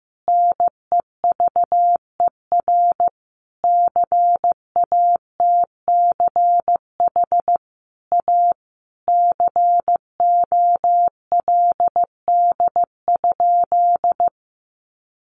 Morse code, heard as a keyed tone, about 15 wpm, 700 Hz. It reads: NEVER CATCH A COLD?
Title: Sentences from Top 1500 Words 15wpm